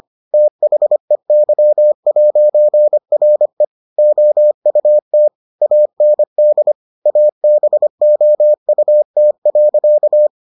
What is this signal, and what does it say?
Morse code, heard as a keyed tone, about 25 wpm, 600 Hz. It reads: THEY'RE OUT AND ABOUT.